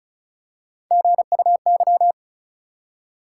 Morse code, heard as a keyed tone, about 36 wpm, 700 Hz. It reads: GUY
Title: Three Letter Words - 35wpm